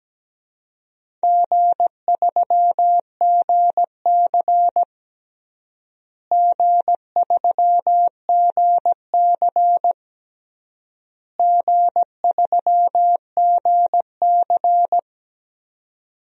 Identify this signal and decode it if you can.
Morse code, heard as a keyed tone, about 17 wpm, 700 Hz. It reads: G3GC G3GC G3GC
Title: DXCC-100 Call Signs - 4 Characters - Repeated 3x - 17wpm